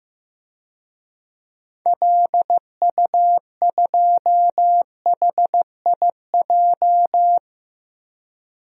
Morse code, heard as a keyed tone, about 15 wpm, 700 Hz. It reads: LU2HIJ
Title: DXCC-100 Call Signs - 6 Characters - 15wpm